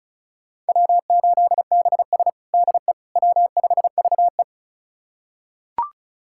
Morse code, heard as a keyed tone, about 36 wpm, 700 Hz. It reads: W8BS DE W5VE
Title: USA Callsign DE Callsign - 4 Characters - 35wpm